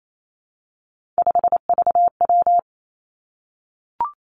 Morse code, heard as a keyed tone, about 28 wpm, 700 Hz. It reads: 5VW
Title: Three Challenging Letter-Number Combinations 28wpm